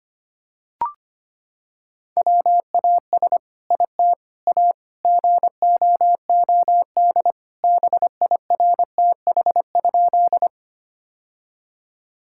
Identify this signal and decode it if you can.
Morse code, heard as a keyed tone, about 25 wpm, 700 Hz. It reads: WAS IT A GOOD BIRTH?
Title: Sentences from Top 1000 Words - Review Part 5 - 25wpm